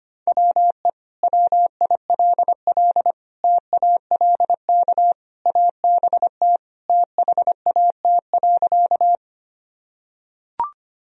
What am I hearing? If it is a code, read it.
Morse code, heard as a keyed tone, about 25 wpm, 700 Hz. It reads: WE WILL TALK ABT THAT.
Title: Sentences from Top 400 Words and Abbreviations 25wpm